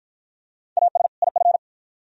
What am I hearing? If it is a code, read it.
Morse code, heard as a keyed tone, about 53 wpm, 700 Hz. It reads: AS IF